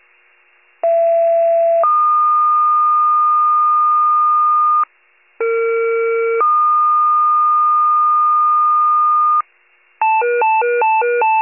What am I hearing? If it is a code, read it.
Two-tone page, heard as a one-second tone, then a three-second tone, about 670 then 1220 Hz, about 460 then 1220 Hz. It is heard from 0.8 to 9.4 s.